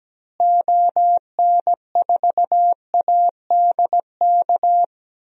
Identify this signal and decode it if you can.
Morse code, heard as a keyed tone, about 17 wpm, 700 Hz. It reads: ON4ADK
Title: DXCC-100 Call Signs - 6 Characters - Repeated 3x - 17wpm